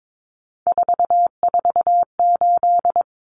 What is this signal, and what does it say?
Morse code, heard as a keyed tone, about 22 wpm, 700 Hz. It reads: 448
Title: Common Cut Numbers - 3-Digit 22wpm